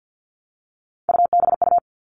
Morse code, heard as a keyed tone, about 50 wpm, 700 Hz. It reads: UBU
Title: All Cut Numbers - 3-Digit 50wpm